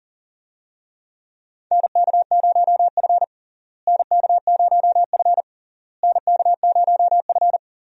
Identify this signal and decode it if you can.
Morse code, heard as a keyed tone, about 40 wpm, 700 Hz. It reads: NK0F NK0F NK0F
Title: USA Call Signs - 4 Characters - Repeated 3x - 40wpm